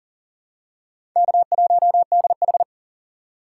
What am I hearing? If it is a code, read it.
Morse code, heard as a keyed tone, about 40 wpm, 700 Hz. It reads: K1DH
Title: USA Call Signs - 3x Character Spacing - 40wpm